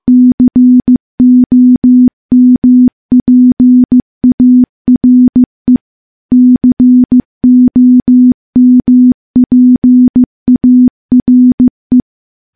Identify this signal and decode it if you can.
Morse code, heard as a keyed tone, about 14 wpm, 250 Hz. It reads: COMPARE COMPARE
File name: morse / morse-training-15WPM